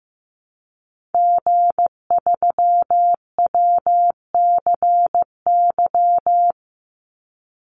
Morse code, heard as a keyed tone, about 15 wpm, 700 Hz. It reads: G3WCY
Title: Call Signs Repeated 3x - 15wpm